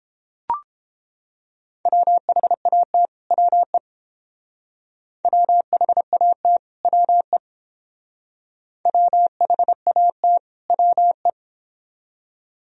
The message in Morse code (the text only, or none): WHAT WE WHAT WE WHAT WE